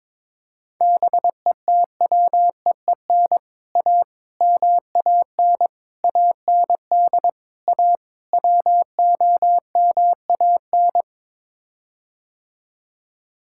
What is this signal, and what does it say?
Morse code, heard as a keyed tone, about 21 wpm, 700 Hz. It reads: BETWEEN A MAN AND A WOMAN
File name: practice / 22wpm / Sets of 6 Words - 22wpm